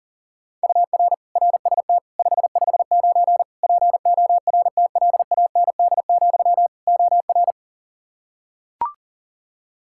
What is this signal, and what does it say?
Morse code, heard as a keyed tone, about 40 wpm, 700 Hz. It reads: UR RST 559 PORTLAND, OR
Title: K1USN SST QSO Practice 40wpm